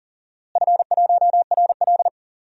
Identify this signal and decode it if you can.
Morse code, heard as a keyed tone, about 40 wpm, 700 Hz. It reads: F1RL
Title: DXCC-100 Call Signs - 4 Characters - Repeated 2x - 40wpm